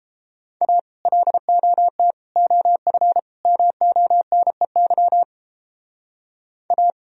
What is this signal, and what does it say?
Morse code, heard as a keyed tone, about 32 wpm, 700 Hz. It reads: A LOT OF MONEY A